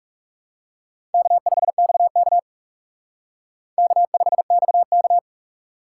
Morse code, heard as a keyed tone, about 38 wpm, 700 Hz. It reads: K5XK K5XK